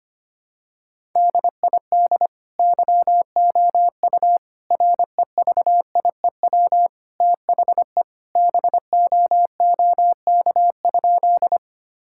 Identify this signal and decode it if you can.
Morse code, heard as a keyed tone, about 25 wpm, 700 Hz. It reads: DID YOU REVIEW THE BOOK?
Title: Sentences from Top 1000 Words - Review Part 6 - 25wpm